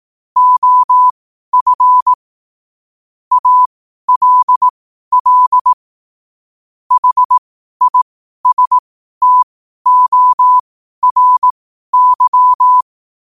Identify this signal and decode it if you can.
Morse code, heard as a keyed tone, about 18 wpm, spaced as for 12 wpm, 1000 Hz. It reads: OF ALL HISTORY
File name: 12wpm